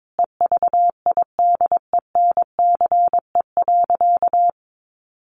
Morse code, heard as a keyed tone, about 22 wpm, 700 Hz. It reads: EVIDENCE.
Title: Sentences from Top 1300 Words 22wpm